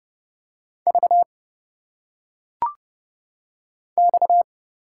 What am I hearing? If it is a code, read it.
Morse code, heard as a keyed tone, about 30 wpm, 700 Hz. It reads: V X
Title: Single Letters - Phonetic Pronunciation 30wpm